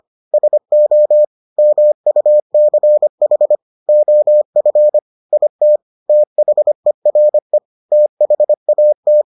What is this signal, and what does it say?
Morse code, heard as a keyed tone, about 25 wpm, 600 Hz. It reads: SO MUCH OF IT THERE THAT